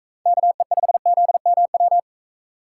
Morse code, heard as a keyed tone, about 42 wpm, 700 Hz. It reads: KE5ZMW